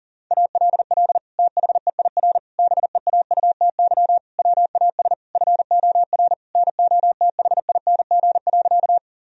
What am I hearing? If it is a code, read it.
Morse code, heard as a keyed tone, about 37 wpm, 700 Hz. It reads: ALL THEIR BEAUTY WAS FOR NOTHING.
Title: Sentences from Top 1000 Words - Review Part 2 - 40wpm